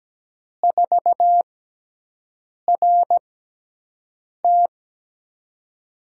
Morse code, heard as a keyed tone, about 17 wpm, spaced as for 6 wpm, 700 Hz. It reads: 4RT
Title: USA Call Signs - 5 Characters - 6x Character Spacing - 17wpm